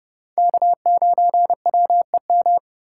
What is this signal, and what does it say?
Morse code, heard as a keyed tone, about 29 wpm, 700 Hz. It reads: K9WEM